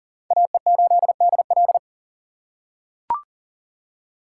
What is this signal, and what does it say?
Morse code, heard as a keyed tone, about 40 wpm, 700 Hz. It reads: AE8DL
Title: USA Call Signs - 5 Characters - 4x Character Spacing - 40wpm